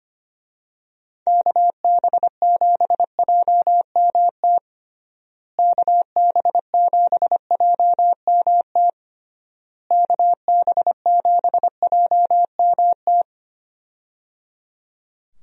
Morse code, heard as a keyed tone, about 25 wpm, 700 Hz. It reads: KB7JMT KB7JMT KB7JMT
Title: USA Call Signs - 6 Characters - Repeated 3x - 25wpm